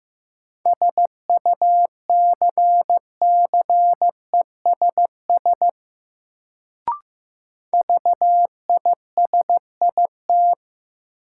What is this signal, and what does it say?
Morse code, heard as a keyed tone, about 15 wpm, 700 Hz. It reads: SUCCESS VISIT
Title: Sentences from Top 900 Words - New Word Drill - 15wpm